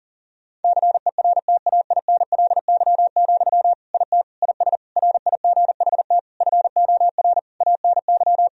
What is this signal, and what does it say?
Morse code, heard as a keyed tone, about 37 wpm, 700 Hz. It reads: CERTAINLY, IT IS RIGHT FOR ANY
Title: Sentences from Top 1000 Words - Review Part 2 - 40wpm